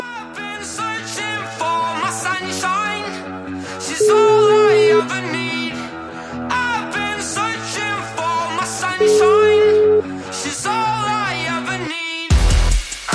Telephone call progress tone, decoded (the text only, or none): ringback tone